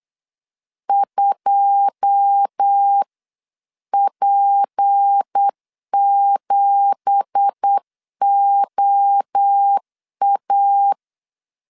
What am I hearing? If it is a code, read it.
Morse code, heard as a keyed tone, about 8 wpm, 800 Hz. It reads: 2P7OA
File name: CW / DET / 1 9 1500C S B 8wpm 5624